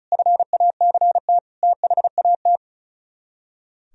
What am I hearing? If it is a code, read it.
Morse code, heard as a keyed tone, about 45 wpm, 700 Hz. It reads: FACT THAT